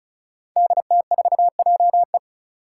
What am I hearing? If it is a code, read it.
Morse code, heard as a keyed tone, about 34 wpm, 700 Hz. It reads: DT4JE
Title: Five Challenging Letter-Number Combinations 35wpm